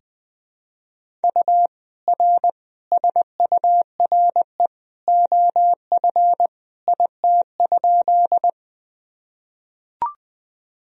Morse code, heard as a keyed tone, about 20 wpm, 700 Hz. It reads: U R SURE OF IT?
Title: Sentences from Top 300 Words and Abbreviations 20wpm